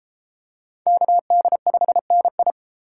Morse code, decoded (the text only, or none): KD5NI